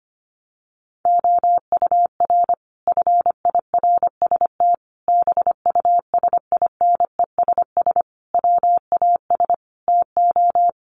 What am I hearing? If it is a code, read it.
Morse code, heard as a keyed tone, about 25 wpm, 700 Hz. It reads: OUR FIRST BUSINESS WAS TO
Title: Sentences from Top 1300 Words 25wpm